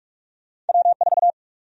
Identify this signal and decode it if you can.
Morse code, heard as a keyed tone, about 45 wpm, 700 Hz. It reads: W4
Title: USA Call Sign Prefixes - 45wpm